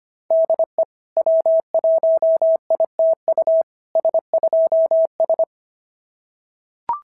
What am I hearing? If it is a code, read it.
Morse code, heard as a keyed tone, about 25 wpm, 660 Hz. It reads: DE W1ITU S2S
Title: SOTA QSO Practice 25wpm